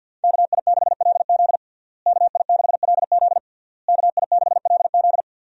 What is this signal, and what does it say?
Morse code, heard as a keyed tone, about 51 wpm, 700 Hz. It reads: KI6LZ KI6LZ KI6LZ
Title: USA Call Signs - 5 Characters - Repeated 3x - 50wpm